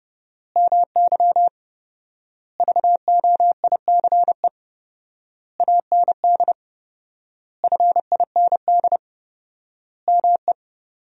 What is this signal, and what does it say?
Morse code, heard as a keyed tone, about 30 wpm, 700 Hz. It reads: MY VOICE AND FIND ME
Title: 4x Word-Spacing - Sentences from Top 500 Words - 30wpm